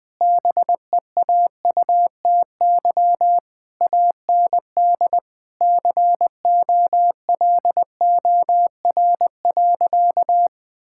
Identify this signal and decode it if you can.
Morse code, heard as a keyed tone, about 20 wpm, 700 Hz. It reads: BEAUTY AND COLOR.
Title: Sentences from Top 1600 Words 20wpm